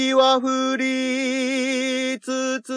There is no code.